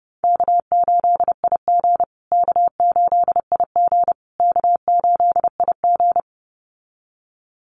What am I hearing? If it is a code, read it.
Morse code, heard as a keyed tone, about 30 wpm, 700 Hz. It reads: K8IG K8IG K8IG
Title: USA Call Signs Repeated 3x 30wpm